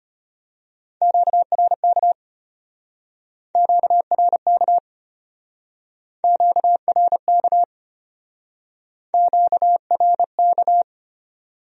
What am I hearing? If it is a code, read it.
Morse code, heard as a keyed tone, about 33 wpm, 700 Hz. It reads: QRK QRK QRK QRK